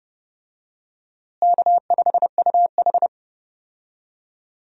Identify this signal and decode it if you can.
Morse code, heard as a keyed tone, about 30 wpm, 700 Hz. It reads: K5UH
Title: USA Call Signs - 4 Characters - Repeated 2x - 30wpm